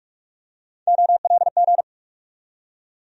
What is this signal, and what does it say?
Morse code, heard as a keyed tone, about 48 wpm, 700 Hz. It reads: OLG
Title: USA Call Sign Postfixes - 3 Letters 45wpm